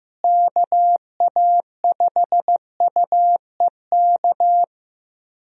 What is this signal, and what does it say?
Morse code, heard as a keyed tone, about 15 wpm, 700 Hz. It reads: KA5UEK
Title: USA Call Signs - 6 Characters - Repeated 3x - 15wpm